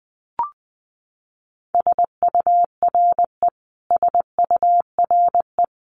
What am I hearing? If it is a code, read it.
Morse code, heard as a keyed tone, about 20 wpm, 700 Hz. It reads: SURE SURE
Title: Top 500 Words Repeated 20wpm